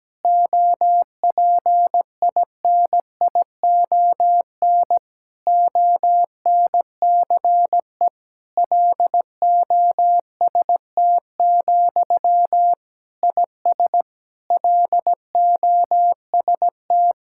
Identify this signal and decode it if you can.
Morse code, heard as a keyed tone, about 17 wpm, 700 Hz. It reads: OPINION ONCE LOST, IS LOST
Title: Sentences from Top 1600 Words 17wpm